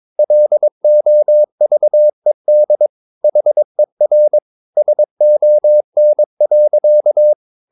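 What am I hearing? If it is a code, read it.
Morse code, heard as a keyed tone, about 22 wpm, 600 Hz. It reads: LOVED HER SON.
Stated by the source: Morse code practice at 22 wpm, sending text from an English novel.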